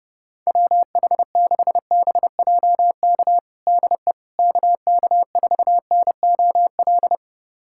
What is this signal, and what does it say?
Morse code, heard as a keyed tone, about 32 wpm, 700 Hz. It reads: WH6BJK DE KK4NOL
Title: USA Callsign DE Callsign 30wpm